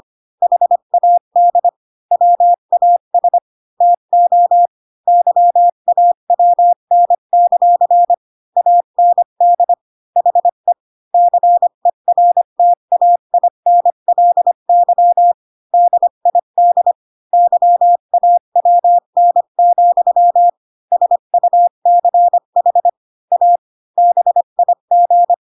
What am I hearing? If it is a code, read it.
Morse code, heard as a keyed tone, about 25 wpm, 700 Hz. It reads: HAD WAS TO YAWN; AND HE CERTAINLY DID YAWN, SUCH A BIG
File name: Pinocchio / Pinocchio ch5 20 25 ik1hge